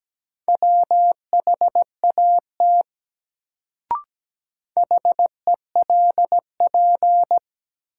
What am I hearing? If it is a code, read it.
Morse code, heard as a keyed tone, about 17 wpm, 700 Hz. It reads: WHAT HELP
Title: Sets of 2 Words - New Word Drill - 17wpm